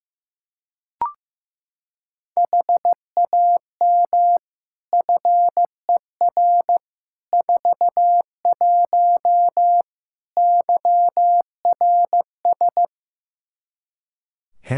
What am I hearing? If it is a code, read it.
Morse code, heard as a keyed tone, about 15 wpm, 700 Hz. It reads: HAM FER 41 YRS
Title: K1USN SST QSO Practice 15wpm